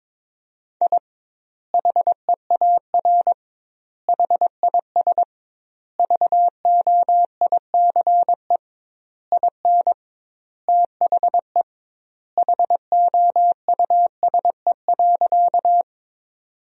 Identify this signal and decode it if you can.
Morse code, heard as a keyed tone, about 22 wpm, 700 Hz. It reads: I HEAR HIS VOICE IN THE HOUSE.
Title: 2x Word-Spacing - Sentences from Top 500 Words - 22wpm